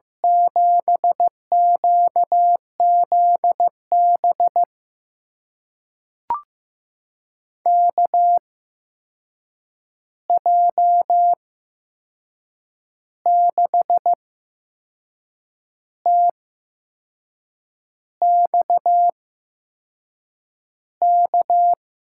Morse code, heard as a keyed tone, about 15 wpm, spaced as for 6 wpm, 700 Hz. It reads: 7QZB KJ6TXK
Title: USA Call Signs - 6 Characters - 8x Character Spacing - 15wpm